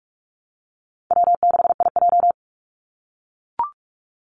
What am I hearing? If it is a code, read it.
Morse code, heard as a keyed tone, about 44 wpm, 700 Hz. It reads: W6IJ
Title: Easy Call Signs - 45wpm